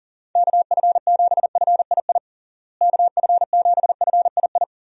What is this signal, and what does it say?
Morse code, heard as a keyed tone, about 40 wpm, 700 Hz. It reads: KF7FII KF7FII
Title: USA Call Signs - 6 Characters - Repeated 2x - 40wpm